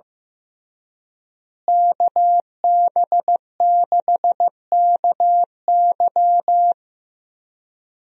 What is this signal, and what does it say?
Morse code, heard as a keyed tone, about 15 wpm, 700 Hz. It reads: KB6KY
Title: USA Call Signs - 5 Characters - Repeated 3x - 15wpm